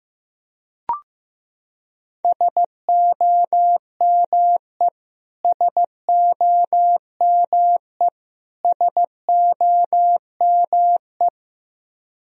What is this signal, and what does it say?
Morse code, heard as a keyed tone, about 15 wpm, 700 Hz. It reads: SOME SOME SOME
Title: Top 100 Words Repeated 15wpm